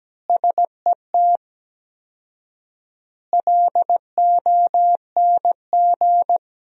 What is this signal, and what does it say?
Morse code, heard as a keyed tone, about 17 wpm, 700 Hz. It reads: SET LONG
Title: 4x Word-Spacing - Sentences from Top 300 Words - 17wpm